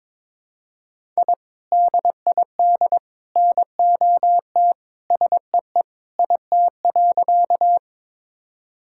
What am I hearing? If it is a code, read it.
Morse code, heard as a keyed tone, about 21 wpm, 700 Hz. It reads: I DID NOT SEE IT.
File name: practice / 22wpm / Sentences with 3-Word Phrases - 22wpm